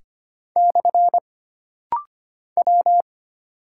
Morse code, heard as a keyed tone, about 25 wpm, 700 Hz. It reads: / W
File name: Single Character (Letter:Number:Punctuation:Prosign) - Rapid-Fire - 25wpm